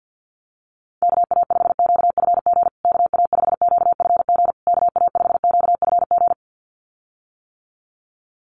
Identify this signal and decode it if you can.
Morse code, heard as a keyed tone, about 50 wpm, 700 Hz. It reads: KA5QFG KA5QFG KA5QFG
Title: USA Call Signs Repeated 3x 50wpm